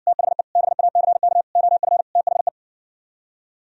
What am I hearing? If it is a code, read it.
Morse code, heard as a keyed tone, about 68 wpm, 700 Hz. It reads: THE BACK OF THE